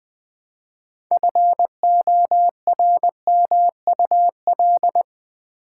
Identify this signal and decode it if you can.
Morse code, heard as a keyed tone, about 20 wpm, 700 Hz. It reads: FORMUL
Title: Build A Word - 3 Letters At A Time - 9-Letter Words - 20wpm